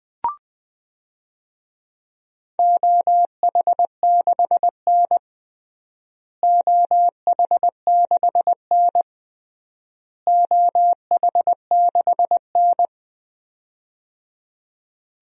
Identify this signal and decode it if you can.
Morse code, heard as a keyed tone, about 20 wpm, 700 Hz. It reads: OH6N OH6N OH6N
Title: DXCC-100 Call Signs - 4 Characters - Repeated 3x - 20wpm